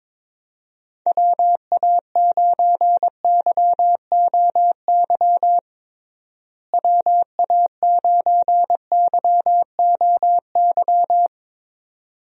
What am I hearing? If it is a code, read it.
Morse code, heard as a keyed tone, about 22 wpm, 700 Hz. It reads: WA9YOY WA9YOY
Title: USA Call Signs - 6 Characters - Repeated 2x - 22wpm